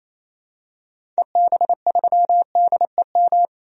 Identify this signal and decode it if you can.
Morse code, heard as a keyed tone, about 28 wpm, 700 Hz. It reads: EB3DEM
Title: DXCC-100 Call Signs - 6 Characters - 28wpm